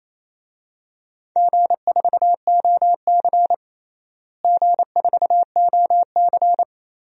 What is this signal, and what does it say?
Morse code, heard as a keyed tone, about 27 wpm, 700 Hz. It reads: G4OC G4OC
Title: DXCC-100 Call Signs - 4 Characters - Repeated 3x - 28wpm